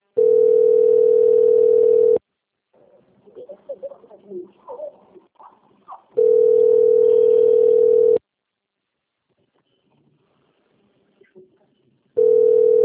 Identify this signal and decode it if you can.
Ringback tone (North America), 440 + 480 Hz.